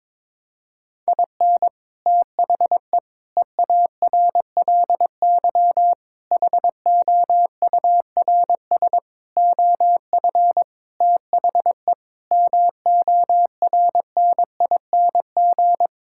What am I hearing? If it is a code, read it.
Morse code, heard as a keyed tone, about 21 wpm, 700 Hz. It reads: IN THE EARLY HOURS OF THE MORNING
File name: Sentences with 7-Word Phrases - 22wpm